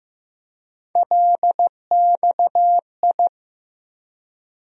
Morse code, heard as a keyed tone, about 15 wpm, 700 Hz. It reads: LXI
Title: USA Call Sign Postfixes - 3 Letters 15wpm